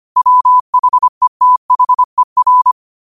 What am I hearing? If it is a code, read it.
Morse code, heard as a keyed tone, about 25 wpm, 1000 Hz. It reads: WHETHER